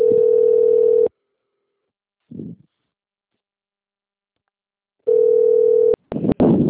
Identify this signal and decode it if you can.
Ringback tone (North America), 440 + 480 Hz.
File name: rg-1600-unknown- - - .7892